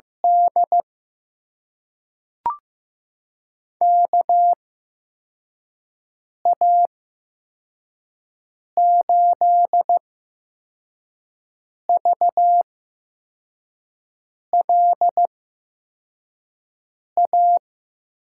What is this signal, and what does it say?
Morse code, heard as a keyed tone, about 15 wpm, spaced as for 4 wpm, 700 Hz. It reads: D KA8VLA
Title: USA Call Signs - 6 Characters - 8x Character Spacing - 15wpm